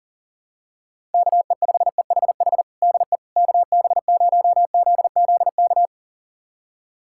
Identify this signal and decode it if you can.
Morse code, heard as a keyed tone, about 40 wpm, 700 Hz. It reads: KE5EHH DE KB0ZZK
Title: USA Callsign DE Callsign - 6 Characters - 40wpm